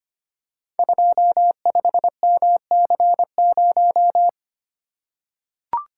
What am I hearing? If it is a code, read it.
Morse code, heard as a keyed tone, about 25 wpm, 700 Hz. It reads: 25MC0
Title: Code Groups - 5 Letter:Numbers - 25wpm